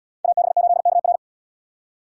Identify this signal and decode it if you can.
Morse code, heard as a keyed tone, about 70 wpm, 700 Hz. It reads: AB8KR